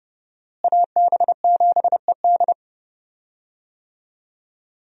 Morse code, heard as a keyed tone, about 30 wpm, 700 Hz. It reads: AB7ED
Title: USA Call Signs - 5 Characters - Repeated 3x - 30wpm